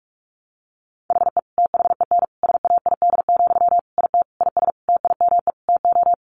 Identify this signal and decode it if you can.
Morse code, heard as a keyed tone, about 44 wpm, 700 Hz. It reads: HE THEN SAID, IT IS TIME TO
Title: Sentences from Top 300 Words 45wpm